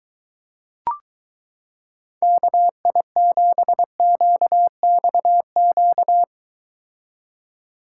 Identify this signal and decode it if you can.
Morse code, heard as a keyed tone, about 22 wpm, 700 Hz. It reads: KI7QXQ